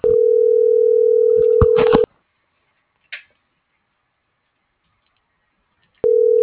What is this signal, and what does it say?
Ringback tone (North America), 440 + 480 Hz.
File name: CENTRAL SOTOVA / out ricardo.lopez